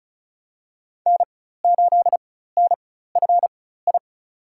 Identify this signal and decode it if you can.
Morse code, heard as a keyed tone, about 36 wpm, spaced as for 16 wpm, 700 Hz. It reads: N8NFI
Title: USA Call Signs - 5 Characters - 4x Character Spacing - 35wpm